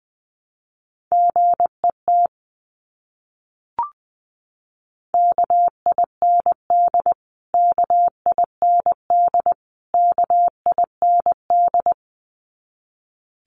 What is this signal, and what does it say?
Morse code, heard as a keyed tone, about 20 wpm, 700 Hz. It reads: GET KIND KIND KIND
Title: Top 500 Words Repeated 20wpm